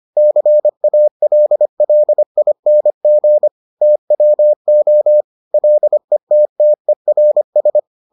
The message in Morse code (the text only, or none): CALLING TWO LETTERS